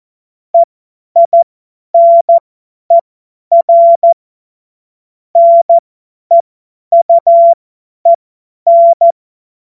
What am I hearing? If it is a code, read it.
Morse code, heard as a keyed tone, about 14 wpm, spaced as for 10 wpm, 680 Hz. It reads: EINER NEUEN
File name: Morsekurs / Witze / Witz 17 Juni 2025 50 F70